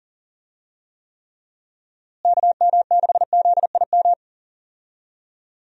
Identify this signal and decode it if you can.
Morse code, heard as a keyed tone, about 40 wpm, 700 Hz. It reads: KM6ZIM
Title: USA Call Signs - 6 Characters - Repeated 2x - 40wpm